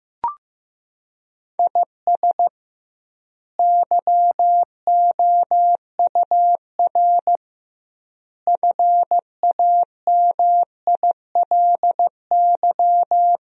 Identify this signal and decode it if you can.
Morse code, heard as a keyed tone, about 15 wpm, 700 Hz. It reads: IS YOUR FAMILY
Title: 2x Word-Spacing - Sentences from Top 400 Words - 15wpm